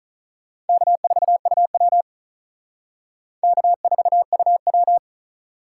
Morse code, heard as a keyed tone, about 45 wpm, 700 Hz. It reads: K4UW K4UW